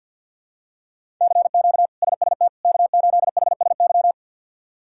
Morse code, heard as a keyed tone, about 49 wpm, 700 Hz. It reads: CQ SST K8HSY
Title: K1USN SST QSO Practice - 50wpm